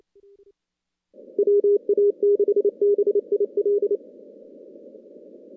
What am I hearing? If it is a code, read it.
Morse code, heard as a keyed tone, about 30 wpm, 410 Hz. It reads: WA6BIL
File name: contests / CW-SS-k5tr- -2100